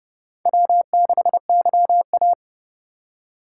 Morse code, heard as a keyed tone, about 36 wpm, 700 Hz. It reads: W6YA
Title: Speed Racing - USA 1x2 Call Signs 30wpm